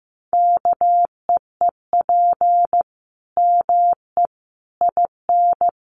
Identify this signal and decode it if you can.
Morse code, heard as a keyed tone, about 15 wpm, 700 Hz. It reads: KEEP ME IN
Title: Sentences from Top 900 Words 15wpm